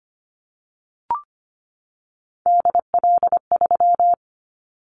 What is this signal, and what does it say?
Morse code, heard as a keyed tone, about 25 wpm, 700 Hz. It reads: DL3
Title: Callsign Prefixes for Top 25 DXCC - 25wpm